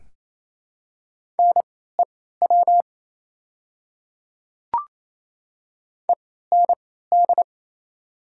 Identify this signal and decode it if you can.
Morse code, heard as a keyed tone, about 28 wpm, spaced as for 16 wpm, 690 Hz. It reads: NEW END
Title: 3x Character Spacing - Top 200 Words - 28wpm